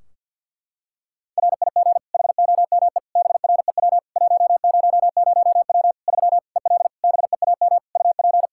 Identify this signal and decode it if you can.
Morse code, heard as a keyed tone, about 52 wpm, 700 Hz. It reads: RIG HOME BREW 100W 3 EL BEAM UP
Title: K1USN SST QSO Practice 50wpm